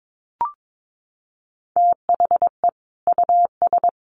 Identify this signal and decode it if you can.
Morse code, heard as a keyed tone, about 22 wpm, 700 Hz. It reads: THE US
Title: Sets of 5 Words Encore2 22wpm